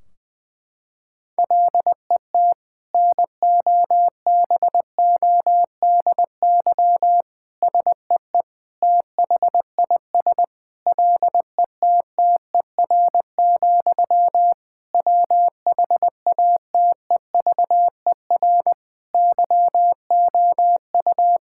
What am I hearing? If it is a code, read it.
Morse code, heard as a keyed tone, about 21 wpm, 700 Hz. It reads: LET NOBODY SEE THIS LETTER, WHATEVER YOU
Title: Sentences from Top 1600 Words 20wpm